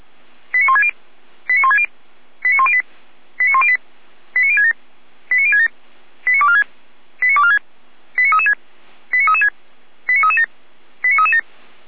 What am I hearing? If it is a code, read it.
Five-tone selective call (ZVEI), heard as quick runs of five tones, tones of about 70 ms. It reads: 89160 89160 89108 89108 89067 89067 89246 89246 89336 89336 89337 89337
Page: SDR/ZVEI